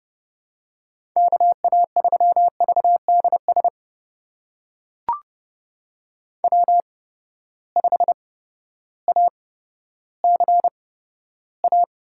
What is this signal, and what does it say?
Morse code, heard as a keyed tone, about 30 wpm, spaced as for 14 wpm, 700 Hz. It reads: KA3VDS W5ACA
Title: USA Call Signs - 8x Character Spacing - 30wpm